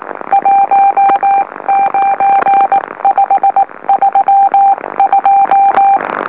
Morse code, heard as a keyed tone, about 19 wpm, spaced as for 16 wpm, 790 Hz. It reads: 19532
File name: M04- -2345utc-LOLO